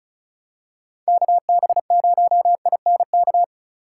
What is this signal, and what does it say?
Morse code, heard as a keyed tone, about 36 wpm, 700 Hz. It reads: KB0INK
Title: USA Call Signs - 4x Character Spacing - 35wpm